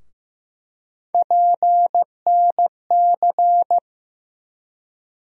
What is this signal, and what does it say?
Morse code, heard as a keyed tone, about 15 wpm, 700 Hz. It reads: PNC
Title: USA Call Sign Postfixes - 3 Letters 15wpm